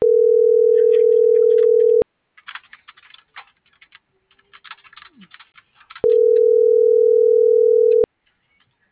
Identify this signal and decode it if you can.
Ringback tone (North America), 440 + 480 Hz.